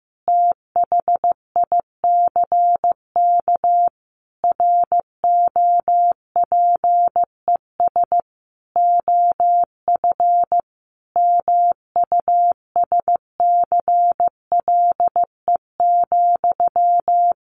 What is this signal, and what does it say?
Morse code, heard as a keyed tone, about 15 wpm, 700 Hz. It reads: THICK ROPES OF MUSCLE,
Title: A Sound Of Thunder 15wpm